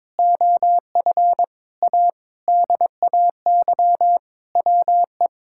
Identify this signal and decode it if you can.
Morse code, heard as a keyed tone, about 22 wpm, 700 Hz. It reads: OF A DAY WE
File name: practice / 22wpm / Sentences with 3-Word Phrases - 22wpm